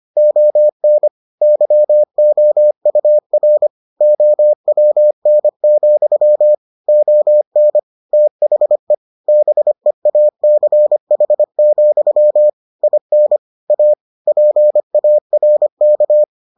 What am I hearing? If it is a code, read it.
Morse code, heard as a keyed tone, about 25 wpm, 600 Hz. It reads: ON YOUR OWN, ON THE BEACH, IN A PARK